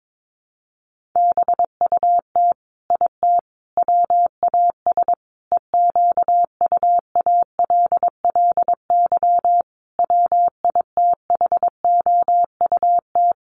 Morse code, heard as a keyed tone, about 22 wpm, 700 Hz. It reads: BUT IT WAS EQUALLY WITHOUT
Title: Sentences from Top 1400 Words 22wpm